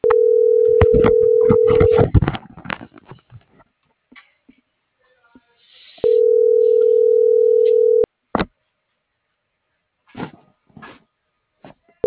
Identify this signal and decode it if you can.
Ringback tone (North America), 440 + 480 Hz.
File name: CENTRAL NAVIERAS / out reyes.ricardo